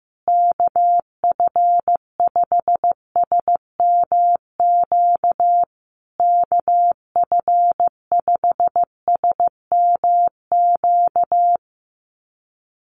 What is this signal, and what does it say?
Morse code, heard as a keyed tone, about 15 wpm, 700 Hz. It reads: KF5SMQ KF5SMQ
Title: USA Call Signs Repeated 3x 15wpm